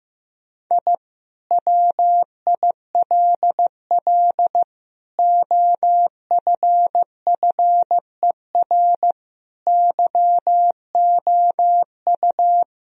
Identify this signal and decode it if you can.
Morse code, heard as a keyed tone, about 16 wpm, 700 Hz. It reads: I WILL OFFER YOU
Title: Sentences from Top 1000 Words - Review Part 4 - 15wpm